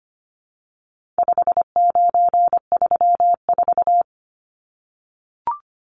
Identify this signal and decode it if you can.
Morse code, heard as a keyed tone, about 25 wpm, 700 Hz. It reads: 5934
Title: Numbers - 4-Digit 25wpm